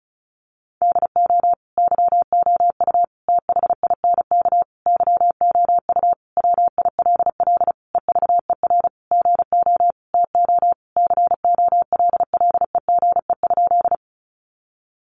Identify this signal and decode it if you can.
Morse code, heard as a keyed tone, about 34 wpm, 700 Hz. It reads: DO YOU THINK YOU WILL EVER GO TO COLLEGE?
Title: Sentences from Top 1300 Words 35wpm